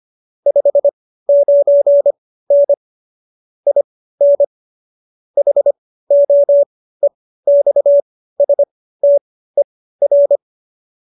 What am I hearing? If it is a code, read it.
Morse code, heard as a keyed tone, about 25 wpm, spaced as for 15 wpm, 570 Hz. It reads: 59N IN HOEXSTER